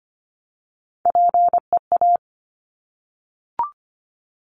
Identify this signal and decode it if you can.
Morse code, heard as a keyed tone, about 25 wpm, 700 Hz. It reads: PEA
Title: Top Three Letter Combinations in Words 25wpm